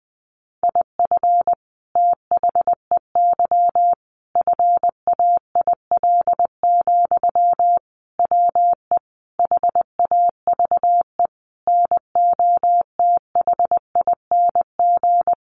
Morse code, heard as a keyed tone, about 20 wpm, 700 Hz. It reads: IF THEY FAIL, WE HAVE NOTHING
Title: Sentences from Top 1100 Words 20wpm